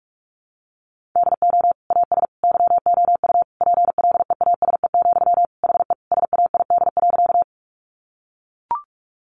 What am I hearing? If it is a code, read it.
Morse code, heard as a keyed tone, about 44 wpm, 700 Hz. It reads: DO AS YOU PLEASE, HE SAID.
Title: Sentences from Top 1500 Words 45wpm